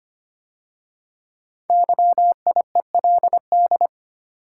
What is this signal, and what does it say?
Morse code, heard as a keyed tone, about 25 wpm, 700 Hz. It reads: YIELD
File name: 5 Letter Words - First 3 Letters Given - 25wpm